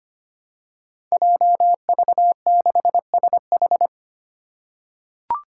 Morse code, heard as a keyed tone, about 25 wpm, 700 Hz. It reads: JV6SH
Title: Five Challenging Letter-Number Combinations 25wpm